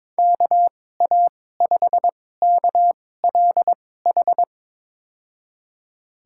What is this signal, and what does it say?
Morse code, heard as a keyed tone, about 22 wpm, spaced as for 16 wpm, 700 Hz. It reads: KA5KLH
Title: USA Call Signs - 6 Characters - 2x Character Spacing - 22wpm